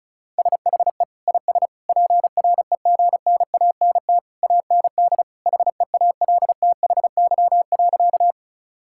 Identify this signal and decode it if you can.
Morse code, heard as a keyed tone, about 36 wpm, 700 Hz. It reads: SHE IS PREGNANT AND HEALTHY.
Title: Sentences from Top 1000 Words - Review Part 6 - 35wpm